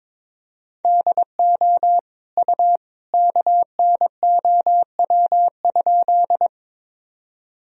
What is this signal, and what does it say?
Morse code, heard as a keyed tone, about 22 wpm, 700 Hz. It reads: DO U KNOW?
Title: Sentences from Top 100 Words and Abbreviations 22wpm